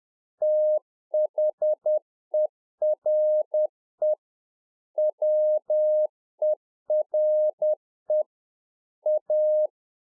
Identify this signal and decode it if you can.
Morse code, heard as a keyed tone, about 10 wpm, 610 Hz. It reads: THERE WERE A